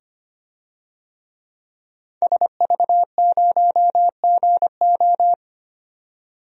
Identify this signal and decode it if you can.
Morse code, heard as a keyed tone, about 25 wpm, 700 Hz. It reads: SV0GO
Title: DXCC-100 Call Signs - 5 Characters - 25wpm